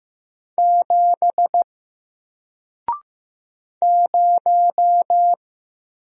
Morse code, heard as a keyed tone, about 15 wpm, 700 Hz. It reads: 7 0
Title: Single Character (Letter:Number:Punctuation:Prosign) Alt 15wpm